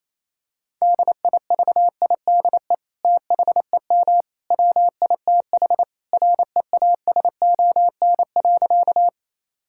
Morse code, heard as a keyed tone, about 28 wpm, 700 Hz. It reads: DIVIDE THEM WITH REASON.
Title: Sentences from Top 1600 Words 28wpm